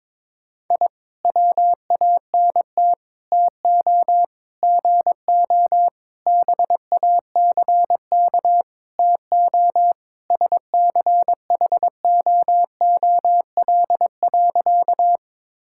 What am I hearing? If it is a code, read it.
Morse code, heard as a keyed tone, about 22 wpm, 700 Hz. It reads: I WANT TO GO BACK TO SCHOOL.
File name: practice / 22wpm / Sentences with 4-Word Phrases - 22wpm